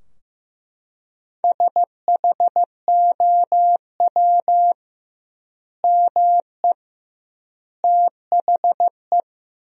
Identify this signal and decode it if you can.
Morse code, heard as a keyed tone, about 15 wpm, 700 Hz. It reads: SHOW ME THE
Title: 2x Word-Spacing - Sentences from Top 200 Words - 15wpm